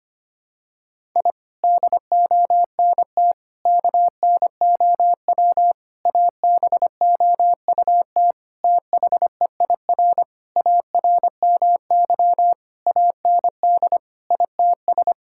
Morse code, heard as a keyed tone, about 25 wpm, 700 Hz. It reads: I DONT KNOW ABOUT THEIR ARMY AND ITS
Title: Sentences from Top 1000 Words - Review Part 5 - 25wpm